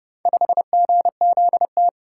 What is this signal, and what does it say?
Morse code, heard as a keyed tone, about 28 wpm, 700 Hz. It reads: 5GZT